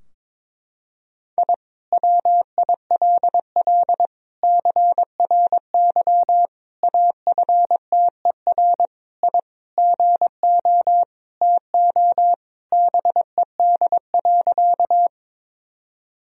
Morse code, heard as a keyed tone, about 21 wpm, 700 Hz. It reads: I WILL CRY AFTER I GO TO BED.
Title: Sentences from Top 1000 Words - Review Part 2 - 22wpm